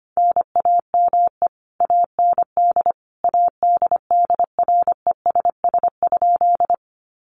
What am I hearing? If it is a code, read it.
Morse code, heard as a keyed tone, about 25 wpm, 700 Hz. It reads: NAME AND ADDRESS?
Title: Sentences from Top 1000 Words 25wpm